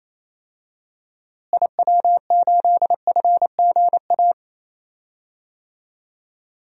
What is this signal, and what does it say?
Morse code, heard as a keyed tone, about 28 wpm, 700 Hz. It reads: IW8FGA
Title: DXCC-100 Call Signs - 6 Characters - Repeated 3x - 28wpm